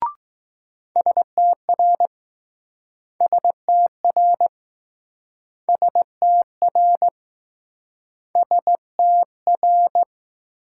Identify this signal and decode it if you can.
Morse code, heard as a keyed tone, about 20 wpm, 700 Hz. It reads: STR STR STR STR